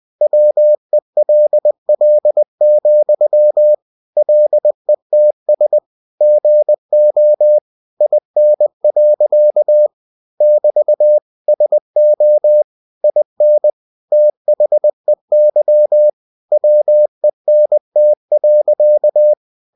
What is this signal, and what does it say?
Morse code, heard as a keyed tone, about 20 wpm, 600 Hz. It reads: WELL, LETS GO IN. <BT> SO IN THEY WENT.
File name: cw-books / winnie-the-pooh / winnie-20wpm-page-0049